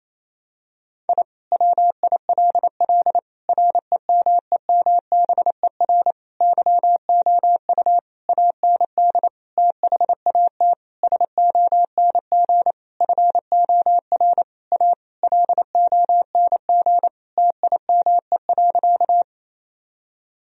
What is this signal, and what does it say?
Morse code, heard as a keyed tone, about 28 wpm, 700 Hz. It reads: I WILL REMEMBER YOU AND THAT SONG FOR A LONG TIME.
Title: Sentences from Top 1000 Words - Review Part 2 - 28wpm